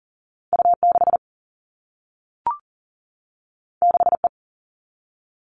Morse code, heard as a keyed tone, about 40 wpm, 700 Hz. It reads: U6 6E
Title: All Cut Numbers - 2-Digit 40wpm